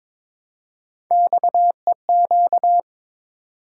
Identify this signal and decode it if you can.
Morse code, heard as a keyed tone, about 22 wpm, 700 Hz. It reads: XEQ
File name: practice / 22wpm / Three Letter-Number Combinations - 22wpm